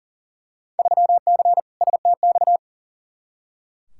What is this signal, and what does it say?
Morse code, heard as a keyed tone, about 40 wpm, 700 Hz. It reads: 3C STX
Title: Field Day Exchanges Encore - 40wpm